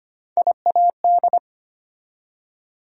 Morse code, heard as a keyed tone, about 25 wpm, 700 Hz. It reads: IAD
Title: USA Call Sign Postfixes - 3 Letters 25wpm